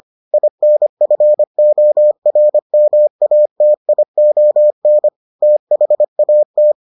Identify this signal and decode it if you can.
Morse code, heard as a keyed tone, about 25 wpm, 600 Hz. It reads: INFORMATION THAT